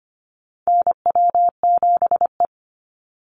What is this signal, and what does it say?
Morse code, heard as a keyed tone, about 25 wpm, 700 Hz. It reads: NW7E
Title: NA SOTA Chaser Callsigns 25wpm